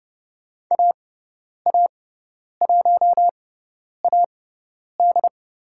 Morse code, heard as a keyed tone, about 30 wpm, spaced as for 10 wpm, 700 Hz. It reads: AA1AD
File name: Lesson 18g - Letter m - Call Signs 30wpm